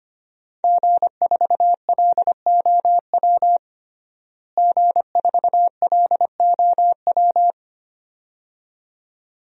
Morse code, heard as a keyed tone, about 25 wpm, 700 Hz. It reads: G4LOW G4LOW
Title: DXCC-100 Call Signs - 5 Characters - Repeated 2x - 25wpm